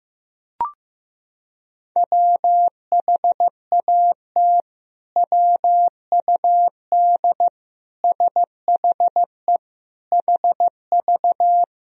Morse code, heard as a keyed tone, about 15 wpm, 700 Hz. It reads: WHAT WUD SHE HV
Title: Sentences from Top 100 Words and Abbreviations 15wpm